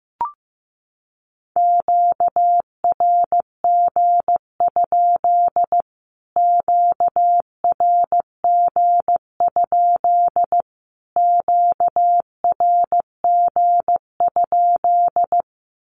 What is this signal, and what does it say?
Morse code, heard as a keyed tone, about 15 wpm, 700 Hz. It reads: QRG? QRG? QRG?
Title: Q-Codes Repeated 3x 15wpm